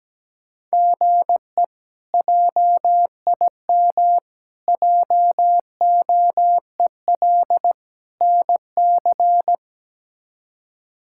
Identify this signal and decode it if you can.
Morse code, heard as a keyed tone, about 17 wpm, 700 Hz. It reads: GE JIM JOEL NC
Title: K1USN SST QSO Practice - 17wpm